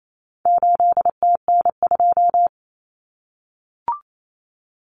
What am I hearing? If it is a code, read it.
Morse code, heard as a keyed tone, about 28 wpm, 700 Hz. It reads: 8TN2
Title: Common Cut Numbers - 4-Digit 28wpm